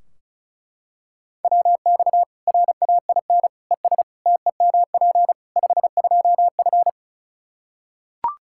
Morse code, heard as a keyed tone, about 36 wpm, 700 Hz. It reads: WX RAIN ES TEMP 52F
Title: Practice QSO - Exchange 2 - Info - 35wpm